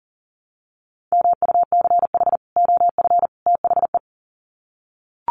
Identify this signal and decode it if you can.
Morse code, heard as a keyed tone, about 40 wpm, 700 Hz. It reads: MUCH OF THE
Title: Sets of 3 Words 40wpm